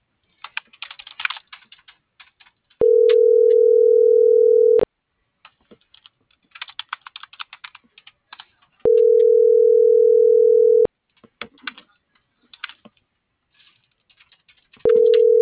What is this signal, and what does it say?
Ringback tone (North America), 440 + 480 Hz.